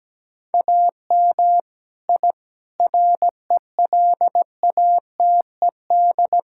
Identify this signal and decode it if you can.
Morse code, heard as a keyed tone, about 17 wpm, 700 Hz. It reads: AM I RELATED
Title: Sentences from Top 1000 Words - Review Part 5 - 17wpm